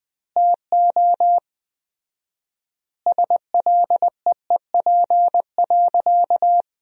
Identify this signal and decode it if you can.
Morse code, heard as a keyed tone, about 20 wpm, 700 Hz. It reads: TO SLEEP.
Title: 4x Word-Spacing - Sentences from Top 500 Words - 20wpm